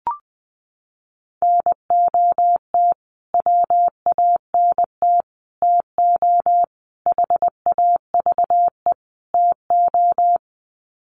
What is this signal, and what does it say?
Morse code, heard as a keyed tone, about 20 wpm, 700 Hz. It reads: NOT WANT TO HAVE TO
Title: Sets of 5 Words Encore2 20wpm